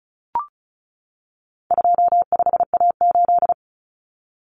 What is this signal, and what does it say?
Morse code, heard as a keyed tone, about 34 wpm, 700 Hz. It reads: 25A8
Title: Common Cut Numbers - 4-Digit 35wpm